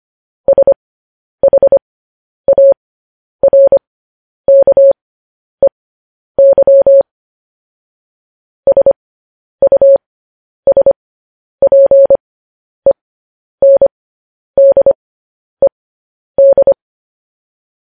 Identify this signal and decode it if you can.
Morse code, heard as a keyed tone, about 25 wpm, spaced as for 10 wpm, 550 Hz. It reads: SHARKEY SUSPENDED
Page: news headlines in morse code at 10 wpm